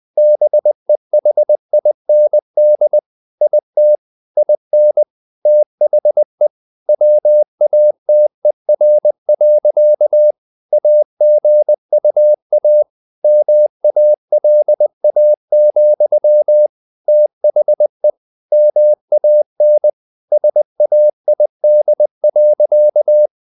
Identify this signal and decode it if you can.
Morse code, heard as a keyed tone, about 20 wpm, 600 Hz. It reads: BEHIND IT IN THE WATER. AGUA MALA, THE MAN SAID.
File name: the old man and the sea / the old man and the sea part 1 20wpm